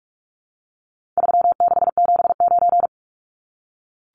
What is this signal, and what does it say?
Morse code, heard as a keyed tone, about 44 wpm, 700 Hz. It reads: 3679
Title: Numbers - 4-Digit 45wpm